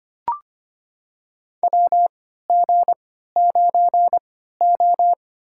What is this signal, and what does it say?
Morse code, heard as a keyed tone, about 25 wpm, spaced as for 14 wpm, 700 Hz. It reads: WG9O
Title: USA Call Signs - 3x Character Spacing - 25wpm